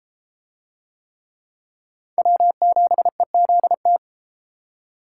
Morse code, heard as a keyed tone, about 33 wpm, 700 Hz. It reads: W7EZT